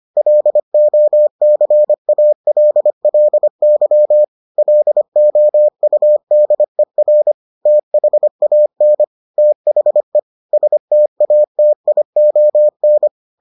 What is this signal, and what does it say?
Morse code, heard as a keyed tone, about 25 wpm, 600 Hz. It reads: LOCALLY LOUDER THAN THE STATION